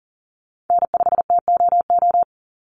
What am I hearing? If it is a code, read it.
Morse code, heard as a keyed tone, about 40 wpm, 700 Hz. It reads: N5TOO
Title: Call Signs - 40wpm